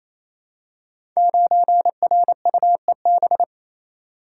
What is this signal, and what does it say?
Morse code, heard as a keyed tone, about 28 wpm, 700 Hz. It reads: 9RUEB
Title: Code Groups - 5 Letter:Numbers - 28wpm